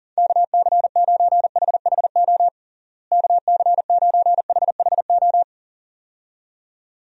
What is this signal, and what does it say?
Morse code, heard as a keyed tone, about 40 wpm, 700 Hz. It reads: KC9HHO KC9HHO
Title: USA Call Signs - 6 Characters - Repeated 2x - 40wpm